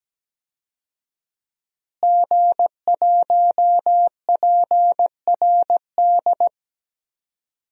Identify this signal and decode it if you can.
Morse code, heard as a keyed tone, about 17 wpm, 700 Hz. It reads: G1PRD
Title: DXCC-100 Call Signs - 5 Characters - Repeated 2x - 17wpm